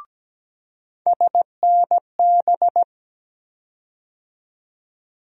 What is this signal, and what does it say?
Morse code, heard as a keyed tone, about 17 wpm, 700 Hz. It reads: SNB
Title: USA Call Sign Postfixes - 3 Letters 17wpm